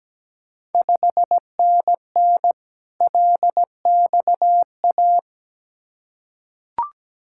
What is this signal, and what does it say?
Morse code, heard as a keyed tone, about 17 wpm, 700 Hz. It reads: 5NN LXA
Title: Ontario QSO Party Exchange 17wpm